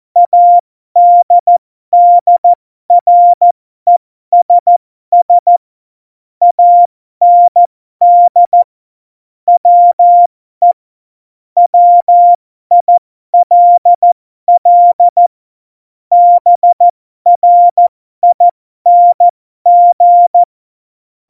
Morse code, heard as a keyed tone, about 14 wpm, spaced as for 12 wpm, 700 Hz. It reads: ADDRESS AND WE WILL BRING
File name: Morsekurs / Witze